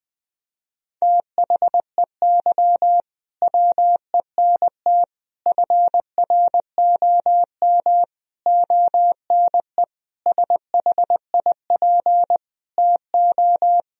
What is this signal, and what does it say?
Morse code, heard as a keyed tone, about 20 wpm, 700 Hz. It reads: THEY WENT FROM ONE SHIP TO
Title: Sentences from Top 1000 Words - Review Part 2 - 20wpm